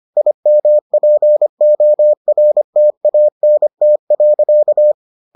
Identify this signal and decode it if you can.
Morse code, heard as a keyed tone, about 25 wpm, 600 Hz. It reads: IMPORTANT.